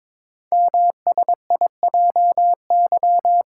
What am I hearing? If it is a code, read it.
Morse code, heard as a keyed tone, about 23 wpm, 700 Hz. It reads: MSIJY
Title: Code Groups - 5 Letters - 22wpm